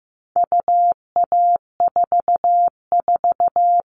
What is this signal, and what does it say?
Morse code, heard as a keyed tone, about 15 wpm, 700 Hz. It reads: UA44
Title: All Cut Numbers - 4-Digit 15wpm